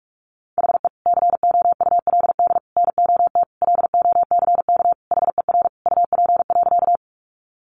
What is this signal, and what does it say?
Morse code, heard as a keyed tone, about 44 wpm, 700 Hz. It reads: HE COULD NOT LOCK HER UP.
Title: Sentences from Top 1300 Words 45wpm